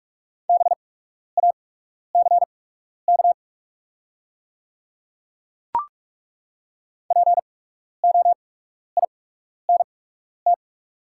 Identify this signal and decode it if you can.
Morse code, heard as a keyed tone, about 48 wpm, spaced as for 12 wpm, 690 Hz. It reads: BACK POINT
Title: 8x Character Spacing - Top 300 Words - 45wpm